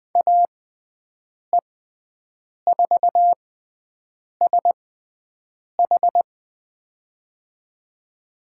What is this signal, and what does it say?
Morse code, heard as a keyed tone, about 20 wpm, spaced as for 7 wpm, 700 Hz. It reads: AE4SH
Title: USA Call Signs - 5 Characters - 6x Character Spacing - 20wpm